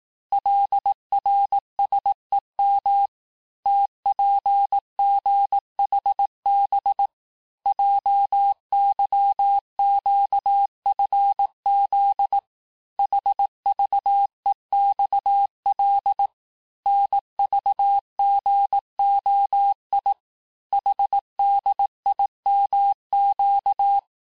Morse code, heard as a keyed tone, about 18 wpm, 800 Hz. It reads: LRSEM TPGHB JYQFZ HVEXL NVGOI HDIMQ
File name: cours-du-lundi files / cours-1852-texte-chiffre-vitesse-17MPM